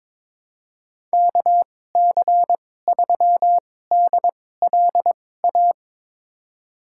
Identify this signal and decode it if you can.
Morse code, heard as a keyed tone, about 22 wpm, spaced as for 16 wpm, 700 Hz. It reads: KC3DLA
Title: USA Call Signs - 6 Characters - 2x Character Spacing - 22wpm